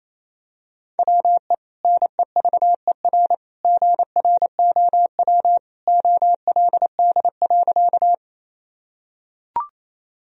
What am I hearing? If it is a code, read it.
Morse code, heard as a keyed tone, about 27 wpm, 700 Hz. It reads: WE NEVER GROW OLD.
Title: Sentences from Top 1000 Words - Review Part 1 - 28wpm